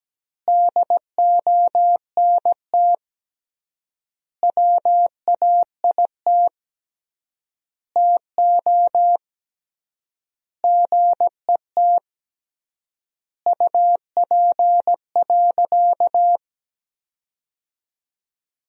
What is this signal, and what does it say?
Morse code, heard as a keyed tone, about 17 wpm, 700 Hz. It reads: DONT WAIT TO GET UP.
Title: 3x Word-Spacing - Sentences from Top 500 Words - 17wpm